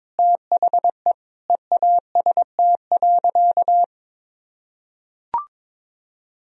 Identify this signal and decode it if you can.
Morse code, heard as a keyed tone, about 22 wpm, 690 Hz. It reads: THE EAST.
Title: Sentences from Top 1000 Words - Review Part 3 - 22wpm